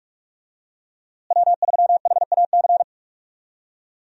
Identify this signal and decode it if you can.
Morse code, heard as a keyed tone, about 41 wpm, 700 Hz. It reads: W3HAC
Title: USA Call Signs - 5 Characters - Repeated 3x - 45wpm